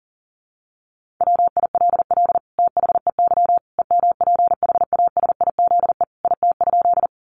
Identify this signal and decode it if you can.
Morse code, heard as a keyed tone, about 40 wpm, 700 Hz. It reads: WILL THEY EMPHASIZE IT?
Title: Sentences from Top 1500 Words 40wpm